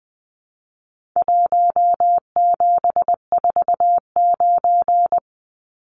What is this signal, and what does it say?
Morse code, heard as a keyed tone, about 20 wpm, 700 Hz. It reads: 1749
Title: Numbers - Contest Style - 4-Digit 20wpm